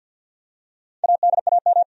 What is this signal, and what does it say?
Morse code, heard as a keyed tone, about 49 wpm, 700 Hz. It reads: ADAM
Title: USA Common Names - 50wpm